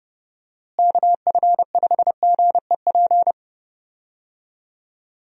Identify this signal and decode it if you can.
Morse code, heard as a keyed tone, about 31 wpm, 700 Hz. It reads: KF5GEP
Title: USA Call Signs - 6 Characters - Repeated 2x - 30wpm